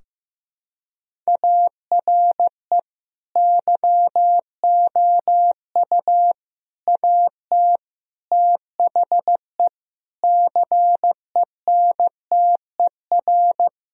Morse code, heard as a keyed tone, about 15 wpm, 700 Hz. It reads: ARE YOU AT THE CENTER
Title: Sentences from Top 1000 Words - Review Part 2 - 15wpm